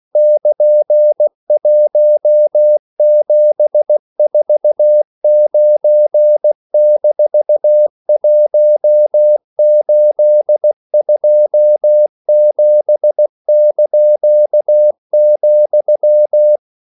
Morse code, heard as a keyed tone, about 16 wpm, 600 Hz. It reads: <KN>1749-1827),
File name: CW / Practice / EnglishQuotations0416